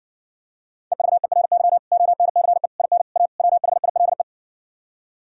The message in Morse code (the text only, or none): EVERY ONCE IN A WHILE